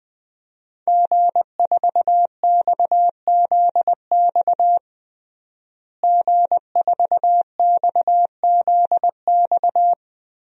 Morse code, heard as a keyed tone, about 20 wpm, 700 Hz. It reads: G4XZX G4XZX
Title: DXCC-100 Call Signs - 5 Characters - Repeated 3x - 20wpm